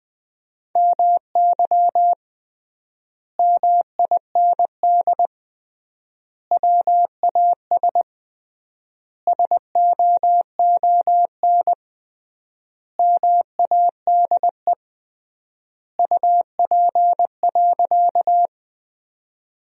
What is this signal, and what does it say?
Morse code, heard as a keyed tone, about 21 wpm, 700 Hz. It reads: MY MIND WAS SOON MADE UP.
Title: 3x Word-Spacing - Sentences from Top 500 Words - 20wpm